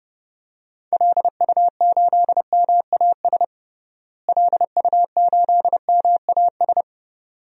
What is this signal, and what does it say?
Morse code, heard as a keyed tone, about 30 wpm, 700 Hz. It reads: LU8MAS LU8MAS
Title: DXCC-100 Call Signs - 6 Characters - Repeated 2x - 30wpm